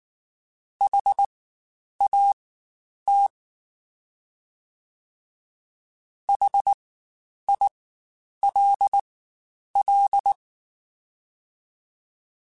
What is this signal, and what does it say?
Morse code, heard as a keyed tone, about 19 wpm, spaced as for 9 wpm, 800 Hz. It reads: HAT HILL
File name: Spaced Alphas / Spaced Alphas mp3s / Hs sp @18